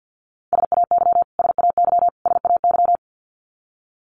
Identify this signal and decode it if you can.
Morse code, heard as a keyed tone, about 50 wpm, 700 Hz. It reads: SAY SAY SAY
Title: Top 200 Words Repeated 50wpm